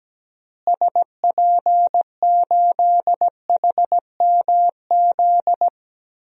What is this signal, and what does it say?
Morse code, heard as a keyed tone, about 17 wpm, 700 Hz. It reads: SP8HMZ
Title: DXCC-100 Call Signs - 6 Characters - Repeated 3x - 17wpm